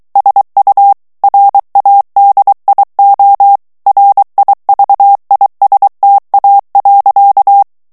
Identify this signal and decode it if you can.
Morse code, heard as a keyed tone, about 23 wpm, 800 Hz. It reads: SU RADIO RIVISTA.
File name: ITALIEN texte clair 2 vitesse 22 mots minute